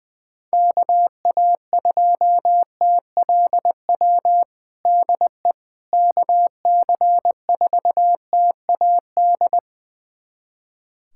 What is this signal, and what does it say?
Morse code, heard as a keyed tone, about 21 wpm, 700 Hz. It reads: KA2TLW DE KC4TAD
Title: USA Callsign DE Callsign - 6 Characters - 20wpm